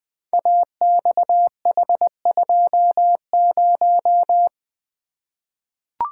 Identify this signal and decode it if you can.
Morse code, heard as a keyed tone, about 20 wpm, 700 Hz. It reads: AXH20
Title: Code Groups - 5 Letter:Numbers - 20wpm